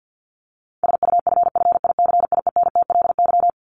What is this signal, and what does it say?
Morse code, heard as a keyed tone, about 49 wpm, 700 Hz. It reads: SUFFICIENTLY
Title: LY Suffix - Top 100 Words with Base Words 50wpm